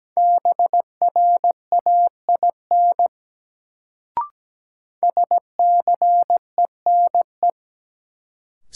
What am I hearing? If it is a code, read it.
Morse code, heard as a keyed tone, about 17 wpm, 700 Hz. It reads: BRAIN SCENE
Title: Sentences from Top 1100 Words - New Word Drill Alt - 17wpm